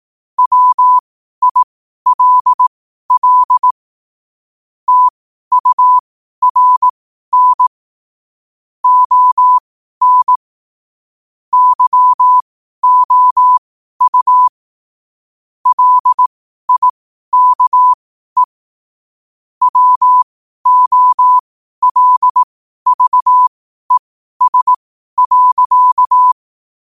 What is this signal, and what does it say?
Morse code, heard as a keyed tone, about 18 wpm, spaced as for 12 wpm, 1000 Hz. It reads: WILL TURN ON YOU LIKE WOLVES.
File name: qotdfiles / 12wpm